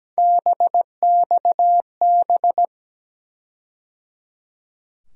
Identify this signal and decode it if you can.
Morse code, heard as a keyed tone, about 17 wpm, 700 Hz. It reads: BXB
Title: USA Call Sign Postfixes - 3 Letters 17wpm